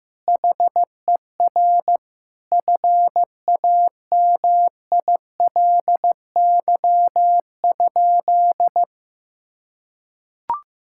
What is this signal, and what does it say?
Morse code, heard as a keyed tone, about 15 wpm, 700 Hz. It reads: HER FAMILY?
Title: Sentences from Top 1000 Words - Review Part 6 - 15wpm